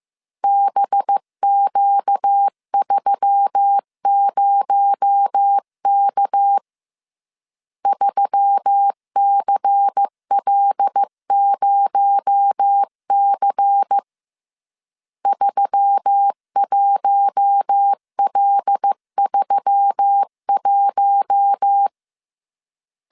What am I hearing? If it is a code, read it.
Morse code, heard as a keyed tone, about 15 wpm, 800 Hz. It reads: BQ30K 3CL0C 31L31
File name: CW / DET / 1 6 1500C S C 14wpm 2951